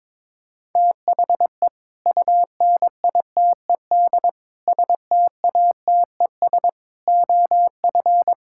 Morse code, heard as a keyed tone, about 22 wpm, 700 Hz. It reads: THE UNITED STATES OF
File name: practice / 22wpm / Build A Phrase - 5-Word Phrases - 22wpm